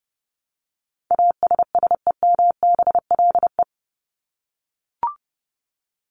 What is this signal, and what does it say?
Morse code, heard as a keyed tone, about 30 wpm, 700 Hz. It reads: ASSEMBLE
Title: LY Suffix - Top 100 Words with Base Words 30wpm